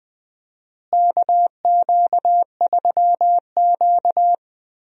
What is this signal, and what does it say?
Morse code, heard as a keyed tone, about 20 wpm, 700 Hz. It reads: KQ3Q
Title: USA Call Signs - 4 Characters - Repeated 3x - 20wpm